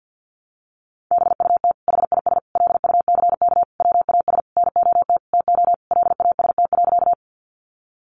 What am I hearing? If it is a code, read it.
Morse code, heard as a keyed tone, about 50 wpm, 700 Hz. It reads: BUT HIS LUCK WAS NOT TO LAST.
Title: Sentences from Top 1500 Words 50wpm